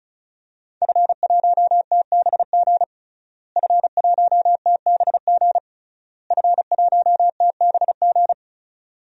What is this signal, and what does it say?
Morse code, heard as a keyed tone, about 36 wpm, 690 Hz. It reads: F1TBG F1TBG F1TBG
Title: DXCC-100 Call Signs - 5 Characters - Repeated 3x - 35wpm